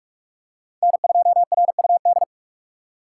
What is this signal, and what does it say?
Morse code, heard as a keyed tone, about 45 wpm, 700 Hz. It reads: N2RUD
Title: USA Call Signs - 5 Characters - 4x Character Spacing - 45wpm